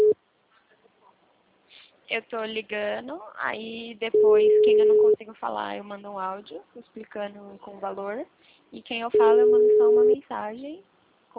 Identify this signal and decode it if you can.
Ringback tone (Europe, 425 Hz), with speech.
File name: out- -2040- - - .48689